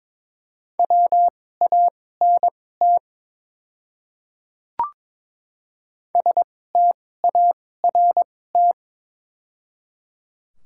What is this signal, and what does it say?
Morse code, heard as a keyed tone, about 22 wpm, spaced as for 16 wpm, 700 Hz. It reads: WANT START
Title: 2x Character Spacing - Top 300 Words - 22wpm